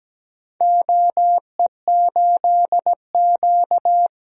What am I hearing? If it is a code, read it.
Morse code, heard as a keyed tone, about 17 wpm, 700 Hz. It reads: OE8Q
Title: DXCC-100 Call Signs - 4 Characters - 17wpm